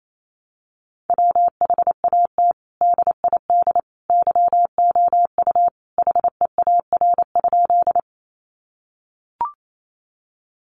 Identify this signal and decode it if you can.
Morse code, heard as a keyed tone, about 28 wpm, 700 Hz. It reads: WHAT DID YOU HEAR?
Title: Sentences from Top 300 Words 28wpm